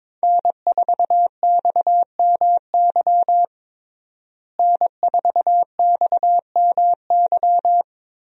Morse code, heard as a keyed tone, about 22 wpm, 700 Hz. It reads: N4XMY N4XMY
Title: USA Call Signs - 5 Characters - Repeated 3x - 22wpm